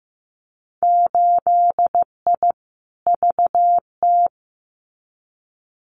Morse code, heard as a keyed tone, about 15 wpm, 700 Hz. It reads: 8I VT
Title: Winter Field Day Exchanges Encore 15wpm